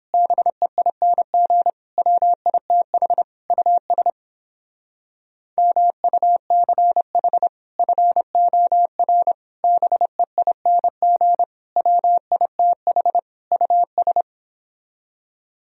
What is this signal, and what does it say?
Morse code, heard as a keyed tone, about 31 wpm, 700 Hz. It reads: BEING WITH US MUCH FOR BEING WITH US